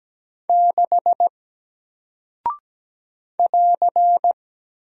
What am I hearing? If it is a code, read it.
Morse code, heard as a keyed tone, about 17 wpm, 700 Hz. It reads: 6 <AR>
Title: Single Character (Letter:Number:Punctuation:Prosign) Alt 17wpm